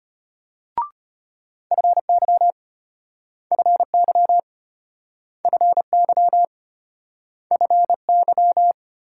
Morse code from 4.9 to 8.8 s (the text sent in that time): FY FY